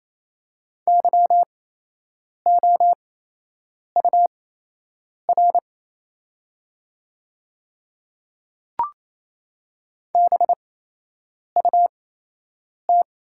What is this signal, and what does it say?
Morse code, heard as a keyed tone, about 28 wpm, spaced as for 8 wpm, 700 Hz. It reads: YOUR BUT
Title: 8x Character Spacing - Top 100 Words - 28wpm